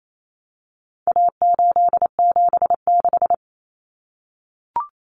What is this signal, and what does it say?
Morse code, heard as a keyed tone, about 28 wpm, 700 Hz. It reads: A876
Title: Common Cut Numbers - 4-Digit 28wpm